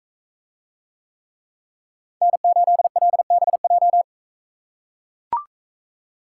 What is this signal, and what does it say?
Morse code, heard as a keyed tone, about 42 wpm, 700 Hz. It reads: N8LBJ